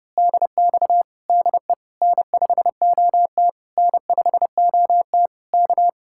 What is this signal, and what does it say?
Morse code, heard as a keyed tone, about 28 wpm, 700 Hz. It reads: DX DE N5OT N5OT K
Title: Speed-Racing Common QSO Phrases Encore - 30wpm